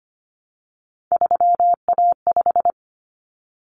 Morse code, heard as a keyed tone, about 25 wpm, 700 Hz. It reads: 3A5
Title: Common Cut Numbers - 3-Digit 25wpm